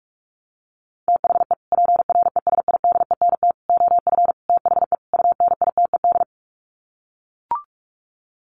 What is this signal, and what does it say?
Morse code, heard as a keyed tone, about 44 wpm, 700 Hz. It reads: THE PRESIDENT OF THE UNITED
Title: Sets of 5 Words 45wpm